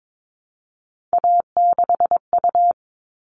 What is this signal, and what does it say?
Morse code, heard as a keyed tone, about 22 wpm, 700 Hz. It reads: A6U
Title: All Cut Numbers - 3-Digit 22wpm